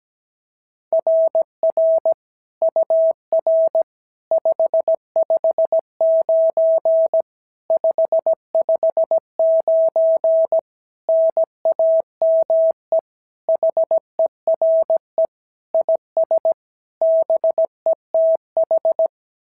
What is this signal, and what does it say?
Morse code, heard as a keyed tone, about 16 wpm, 650 Hz. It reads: RR UR 559 559 NAME HERE IS BETH
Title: SOTA QSO Practice 17wpm